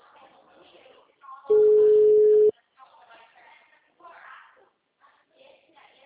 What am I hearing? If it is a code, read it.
Ringback tone (Europe, 425 Hz), with speech.